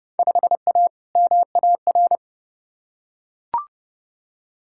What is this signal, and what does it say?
Morse code, heard as a keyed tone, about 32 wpm, 700 Hz. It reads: 5A MAR
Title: Field Day Exchanges Encore - 30wpm